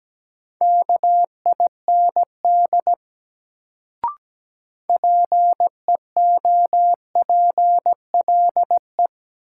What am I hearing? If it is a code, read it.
Morse code, heard as a keyed tone, about 17 wpm, 700 Hz. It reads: KIND PEOPLE
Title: Sentences from Top 200 Words - New Word Drill - 17wpm